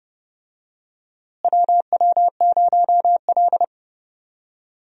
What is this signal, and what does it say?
Morse code, heard as a keyed tone, about 32 wpm, 700 Hz. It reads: WW0L
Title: USA Call Signs - 4 Characters - Repeated 2x - 30wpm